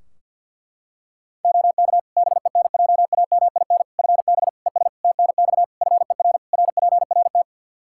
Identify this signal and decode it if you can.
Morse code, heard as a keyed tone, about 52 wpm, 700 Hz. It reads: OK BENJAMIN FB ES TNX FER RPRT
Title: K1USN SST QSO Practice 50wpm